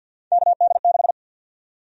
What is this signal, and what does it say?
Morse code, heard as a keyed tone, about 53 wpm, 700 Hz. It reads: KD6
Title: USA Call Sign Prefixes - 50wpm